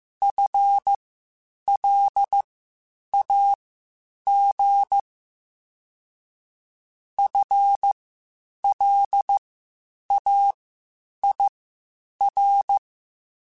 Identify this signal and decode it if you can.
Morse code, heard as a keyed tone, about 15 wpm, spaced as for 8 wpm, 800 Hz. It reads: FLAG FLAIR
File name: Spaced / FL Runs sp mp3s / FL Runs sp @13